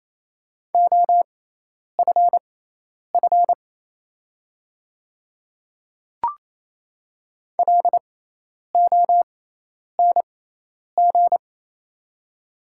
Morse code, heard as a keyed tone, about 27 wpm, spaced as for 10 wpm, 690 Hz. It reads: OFF LONG
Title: 6x Character Spacing - Top 300 Words - 28wpm